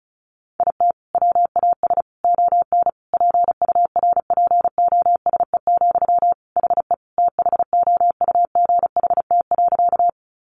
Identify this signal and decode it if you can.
Morse code, heard as a keyed tone, about 34 wpm, 700 Hz. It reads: IT WAS ON PURPOSE, HE THOUGHT.
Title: Sentences from Top 800 Words 35wpm